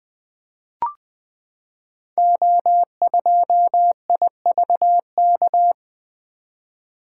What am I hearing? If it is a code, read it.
Morse code, heard as a keyed tone, about 20 wpm, 700 Hz. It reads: O2IVK
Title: Code Groups - 5 Letter:Numbers - 20wpm